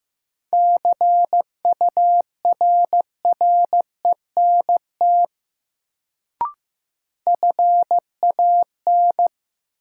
Morse code, heard as a keyed tone, about 16 wpm, 700 Hz. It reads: CURRENT FAN
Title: Sentences from Top 1000 Words - Review Part 4 - New Word Drill - 15wpm